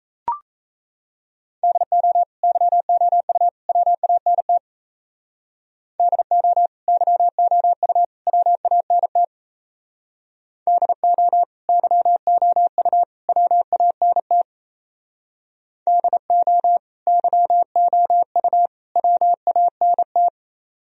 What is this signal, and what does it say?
Morse code, heard as a keyed tone, about 32 wpm, 700 Hz. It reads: DO YOU WANT DO YOU WANT DO YOU WANT DO YOU WANT
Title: Speed-Racing - Sets of 3 Words 28wpm